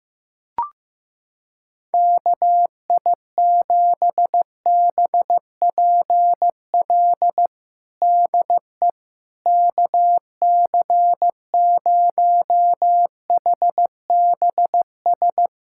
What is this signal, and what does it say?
Morse code, heard as a keyed tone, about 15 wpm, 700 Hz. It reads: KI7BPL DE KC0HBS
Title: USA Callsign DE Callsign - 6 Characters - 15wpm